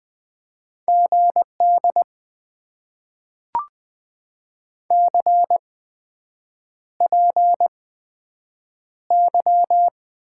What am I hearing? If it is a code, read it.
Morse code, heard as a keyed tone, about 20 wpm, spaced as for 7 wpm, 700 Hz. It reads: GD CPY
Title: Top 90 QSO Elements - 8x Character Spacing - 20wpm